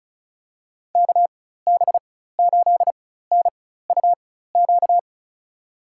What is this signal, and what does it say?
Morse code, heard as a keyed tone, about 36 wpm, spaced as for 16 wpm, 700 Hz. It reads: KB8NUQ
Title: USA Call Signs - 6 Characters - 4x Character Spacing - 35wpm